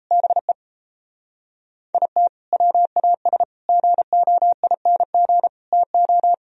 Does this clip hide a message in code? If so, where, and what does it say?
Yes: Morse code from 0.1 to 6.3 s, heard as a keyed tone, about 36 wpm, 700 Hz. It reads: BE IT WAS GOING TO